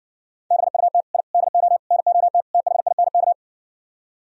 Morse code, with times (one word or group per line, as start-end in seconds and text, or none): 0.5-1.0 BUT
1.1-1.2 I
1.3-1.8 DO
1.9-2.4 NOT
2.5-3.3 THINK